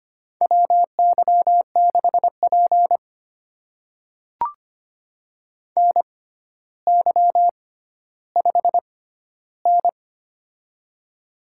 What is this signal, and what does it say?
Morse code, heard as a keyed tone, about 25 wpm, spaced as for 13 wpm, 700 Hz. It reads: WY6P NY5N
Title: USA Call Signs - 4 Characters - 6x Character Spacing - 25wpm